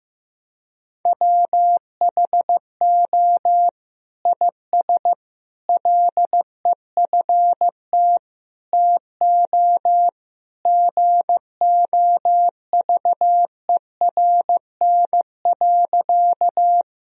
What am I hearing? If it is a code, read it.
Morse code, heard as a keyed tone, about 15 wpm, 700 Hz. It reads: WHO IS LEFT TO GOVERN.
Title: Sentences from Top 1000 Words - Review Part 2 - 15wpm